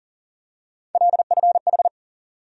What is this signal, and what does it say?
Morse code, heard as a keyed tone, about 40 wpm, 700 Hz. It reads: LFH
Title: USA Call Sign Postfixes - 3 Letters 40wpm